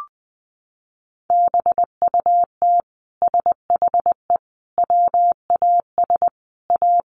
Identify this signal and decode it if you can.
Morse code, heard as a keyed tone, about 20 wpm, 700 Hz. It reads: BUT SHE WAS A
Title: Sentences from Top 1400 Words 20wpm